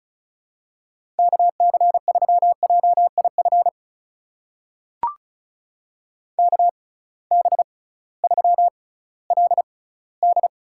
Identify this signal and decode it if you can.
Morse code, heard as a keyed tone, about 35 wpm, spaced as for 19 wpm, 690 Hz. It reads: KC3JIF KB3LD
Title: USA Call Signs - 6 Characters - 6x Character Spacing - 35wpm